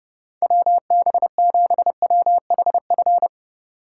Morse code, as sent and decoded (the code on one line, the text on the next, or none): .-- -... --... .-- .... ..-.
WB7WHF